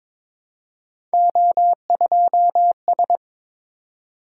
Morse code, heard as a keyed tone, about 23 wpm, 700 Hz. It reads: O2S
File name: practice / 22wpm / Three Letter-Number Combinations - 22wpm